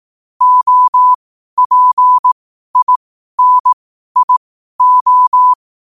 Morse code, heard as a keyed tone, about 18 wpm, spaced as for 12 wpm, 1000 Hz. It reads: OPINIO
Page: Quotes for Wed, 13 Aug 2025 in Morse Code at 12 words per minute.